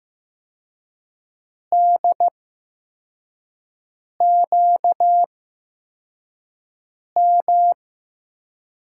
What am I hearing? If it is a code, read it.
Morse code, heard as a keyed tone, about 15 wpm, spaced as for 4 wpm, 700 Hz. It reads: DQM
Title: USA Call Signs - 6 Characters - 8x Character Spacing - 15wpm